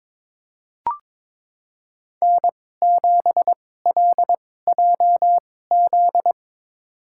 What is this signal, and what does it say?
Morse code, heard as a keyed tone, about 22 wpm, spaced as for 16 wpm, 700 Hz. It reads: N7LJZ
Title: USA Call Signs - 2x Character Spacing - 22wpm